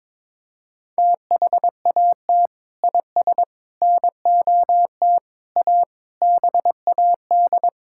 Morse code, heard as a keyed tone, about 22 wpm, 700 Hz. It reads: THAT IS NOT A BAD
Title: Sentences from Top 1600 Words 22wpm